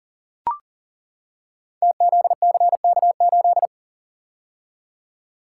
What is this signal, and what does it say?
Morse code, heard as a keyed tone, about 38 wpm, 690 Hz. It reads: TZCK8
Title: Code Groups - 5 Letter:Numbers - 40wpm